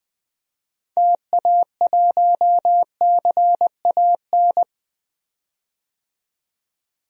Morse code, heard as a keyed tone, about 20 wpm, 700 Hz. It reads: TA1CAN
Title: DXCC-100 Call Signs - 6 Characters - 20wpm